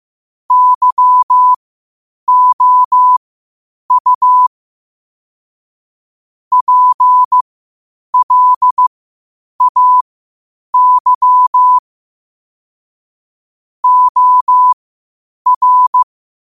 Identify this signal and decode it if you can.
Morse code, heard as a keyed tone, about 15 wpm, spaced as for 8 wpm, 1000 Hz. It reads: YOU PLAY OR
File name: 8wpm